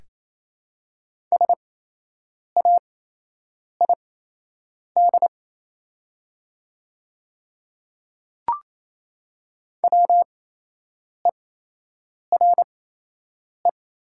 Morse code, heard as a keyed tone, about 28 wpm, spaced as for 8 wpm, 700 Hz. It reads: SAID WERE
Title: 8x Character Spacing - Top 100 Words - 28wpm